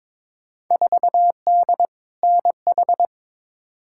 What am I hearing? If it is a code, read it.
Morse code, heard as a keyed tone, about 22 wpm, 700 Hz. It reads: 4D NH
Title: Field Day Exchanges Encore - 22wpm